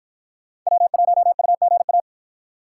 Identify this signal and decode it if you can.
Morse code, heard as a keyed tone, about 53 wpm, 700 Hz. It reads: W1UGA